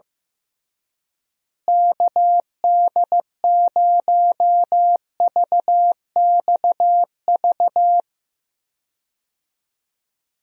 Morse code, heard as a keyed tone, about 15 wpm, 700 Hz. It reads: KD0VXV
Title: USA Call Signs - 6 Characters - Repeated 3x - 15wpm